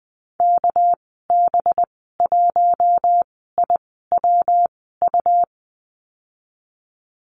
Morse code, heard as a keyed tone, about 19 wpm, spaced as for 14 wpm, 700 Hz. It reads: KB1IWU
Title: 238 Lesson 38g - Letter q - Call Signs 20wpm